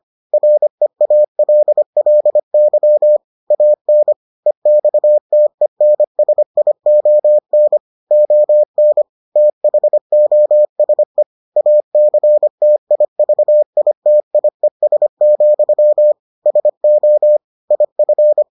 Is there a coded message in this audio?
Yes: Morse code from 0.3 to 18.4 s, heard as a keyed tone, about 25 wpm, 600 Hz. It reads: REALLY AN EXTENSION ON THOSE ACTIVITIES, SO IF